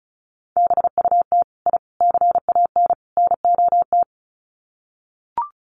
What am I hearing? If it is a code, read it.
Morse code, heard as a keyed tone, about 34 wpm, 700 Hz. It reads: BUT I CAN NOT
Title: Sets of 4 Words 35wpm